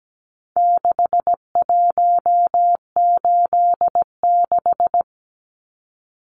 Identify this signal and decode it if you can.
Morse code, heard as a keyed tone, about 17 wpm, 700 Hz. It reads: 6186
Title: Numbers - 4-Digit 17wpm